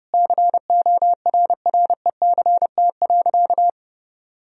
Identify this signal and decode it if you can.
Morse code, heard as a keyed tone, about 30 wpm, 700 Hz. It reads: CORRECT.
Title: Sentences from Top 1000 Words - Review Part 2 - 30wpm